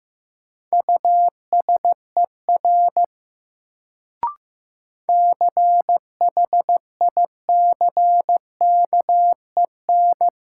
Morse code, heard as a keyed tone, about 15 wpm, 700 Hz. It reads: USER CHICKEN
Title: Sentences from Top 1000 Words - Review Part 5 - New Word Drill - 15wpm